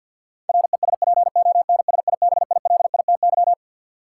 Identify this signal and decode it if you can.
Morse code, heard as a keyed tone, about 53 wpm, 700 Hz. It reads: RESPONSIBILITY
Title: Sentences from Top 1200 Words - New Word Drill Alt - 50wpm